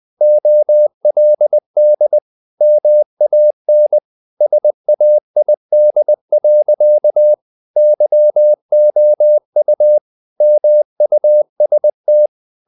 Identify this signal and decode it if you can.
Morse code, heard as a keyed tone, about 20 wpm, 600 Hz. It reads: OLD MAN SAID. YOU MUST
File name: the old man and the sea part 2 20wpm